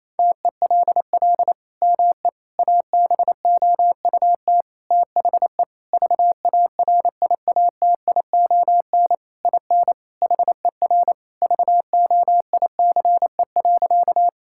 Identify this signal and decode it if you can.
Morse code, heard as a keyed tone, about 28 wpm, 700 Hz. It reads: TELL ME ABOUT THE VARIATION IN HER VOICE.
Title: Sentences from Top 1600 Words 28wpm